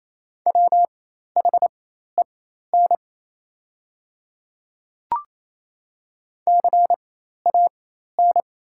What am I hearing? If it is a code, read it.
Morse code, heard as a keyed tone, about 28 wpm, spaced as for 13 wpm, 700 Hz. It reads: WHEN CAN
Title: 4x Character Spacing - Top 100 Words - 28wpm